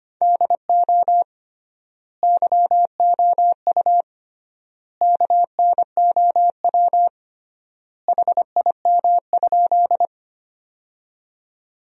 Morse code, heard as a keyed tone, about 25 wpm, 700 Hz. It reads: DO YOU KNOW HIM?
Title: 3x Word-Spacing - Sentences from Top 100 Words - 25wpm